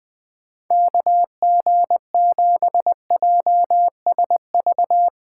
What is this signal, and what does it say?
Morse code, heard as a keyed tone, about 20 wpm, 700 Hz. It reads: KG7JSV